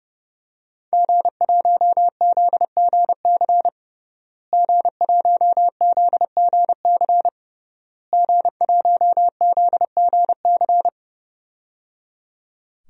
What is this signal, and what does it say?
Morse code, heard as a keyed tone, about 30 wpm, 700 Hz. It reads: G1ZGC G1ZGC G1ZGC
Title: DXCC-100 Call Signs - 5 Characters - Repeated 3x - 30wpm